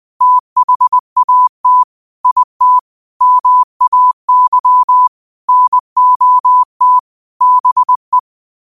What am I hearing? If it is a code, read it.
Morse code, heard as a keyed tone, about 20 wpm, 1000 Hz. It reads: THAT IT MAY NOT BE
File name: qotdfiles / 20wpm